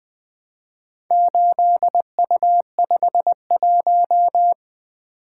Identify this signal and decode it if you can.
Morse code, heard as a keyed tone, about 20 wpm, 700 Hz. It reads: 8U51
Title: Four Letter-Number Combinations - 20wpm